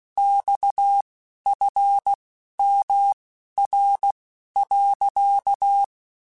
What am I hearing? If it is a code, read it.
Morse code, heard as a keyed tone, about 16 wpm, spaced as for 11 wpm, 800 Hz. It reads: XFMR.
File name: Technical Subjects / Technical Subjects-mp3s / Sw Pwr Sup @10